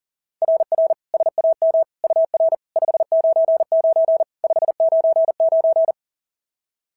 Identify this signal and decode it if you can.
Morse code, heard as a keyed tone, about 40 wpm, 690 Hz. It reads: RR SAM UR 599 599
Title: SOTA QSO Practice 40wpm